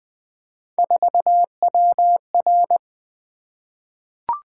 Morse code, heard as a keyed tone, about 20 wpm, 700 Hz. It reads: 4WR
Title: Three Letter-Number Combinations - 20wpm